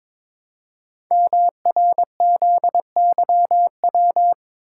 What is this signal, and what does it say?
Morse code, heard as a keyed tone, about 22 wpm, 700 Hz. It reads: MRZYW
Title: Code Groups - 5 Letters - 22wpm